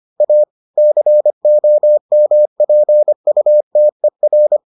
Morse code, heard as a keyed tone, about 25 wpm, 600 Hz. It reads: A COMPUTER